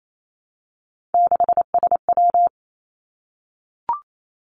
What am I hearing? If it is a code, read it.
Morse code, heard as a keyed tone, about 28 wpm, 700 Hz. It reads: 6SW
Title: Three Challenging Letter-Number Combinations 28wpm